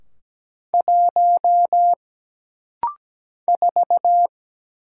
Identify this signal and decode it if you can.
Morse code, heard as a keyed tone, about 17 wpm, 700 Hz. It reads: 1 4
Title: Single Letter-Number Alt - Rapid-Fire 17wpm